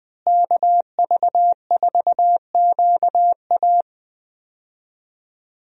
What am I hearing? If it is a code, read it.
Morse code, heard as a keyed tone, about 20 wpm, 700 Hz. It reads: KV4QA
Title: USA Call Signs - 5 Characters - Repeated 3x - 20wpm